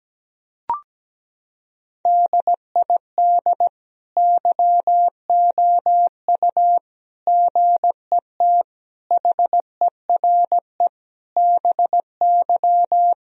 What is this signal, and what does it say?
Morse code, heard as a keyed tone, about 17 wpm, 700 Hz. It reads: DID YOU GET HERE BY
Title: Sentences from Top 1000 Words - Review Part 3 - 17wpm